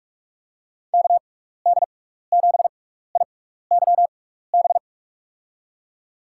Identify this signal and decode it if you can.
Morse code, heard as a keyed tone, about 46 wpm, spaced as for 16 wpm, 700 Hz. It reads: KD7IYB
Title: USA Call Signs - 6 Characters - 6x Character Spacing - 45wpm